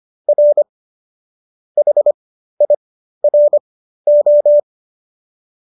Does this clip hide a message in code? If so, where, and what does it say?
Yes: Morse code from 0.3 to 4.6 s, heard as a keyed tone, about 25 wpm, spaced as for 13 wpm, 570 Hz. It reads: R HIRO